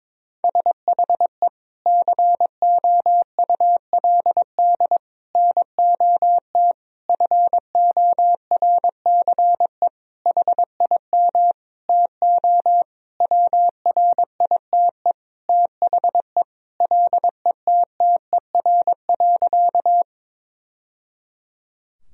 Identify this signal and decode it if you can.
Morse code, heard as a keyed tone, about 22 wpm, 700 Hz. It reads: SHE COULD NOT FORCE HIM TO WRITE THE LETTER.
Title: Sentences from Top 1000 Words - Review Part 3 - 22wpm